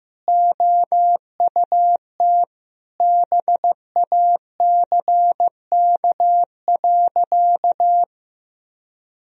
Morse code, heard as a keyed tone, about 15 wpm, 700 Hz. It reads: OUT BACK.
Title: Sentences from Top 1000 Words - Review Part 2 - 15wpm